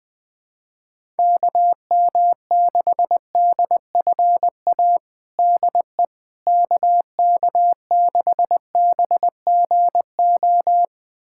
Morse code, heard as a keyed tone, about 19 wpm, 700 Hz. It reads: KM6DFA DE KK6BGO
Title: USA Callsign DE Callsign - 6 Characters - 20wpm